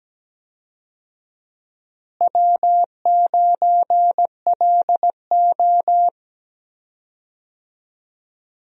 Morse code, heard as a keyed tone, about 17 wpm, 700 Hz. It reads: W9LO
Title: USA Call Signs - 4 Characters - Repeated 2x - 17wpm